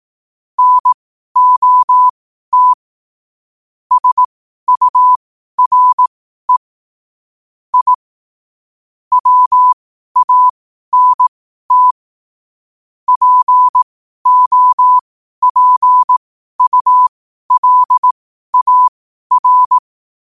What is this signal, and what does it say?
Morse code, heard as a keyed tone, about 18 wpm, spaced as for 12 wpm, 1000 Hz. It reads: NOT SURE I WANT POPULAR
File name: 12 wpm